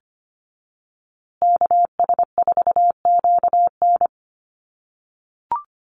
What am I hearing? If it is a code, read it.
Morse code, heard as a keyed tone, about 25 wpm, 700 Hz. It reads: KS4QN
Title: USA Call Signs Repeated 3x 25wpm